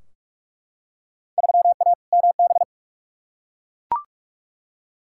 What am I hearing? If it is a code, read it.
Morse code, heard as a keyed tone, about 42 wpm, 700 Hz. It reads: 3A MB
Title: Field Day Exchanges Encore - 45wpm